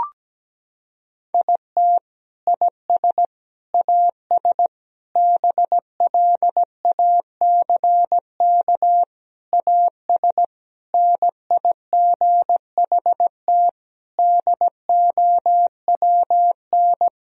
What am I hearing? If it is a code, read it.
Morse code, heard as a keyed tone, about 17 wpm, 700 Hz. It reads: IT IS AS BLACK AS NIGHT DOWN
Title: Sentences from Top 1000 Words - Review Part 2 - 17wpm